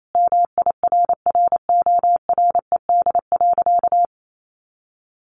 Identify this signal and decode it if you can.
Morse code, heard as a keyed tone, about 28 wpm, 700 Hz. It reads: MIRRORED.
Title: A Sound Of Thunder 28wpm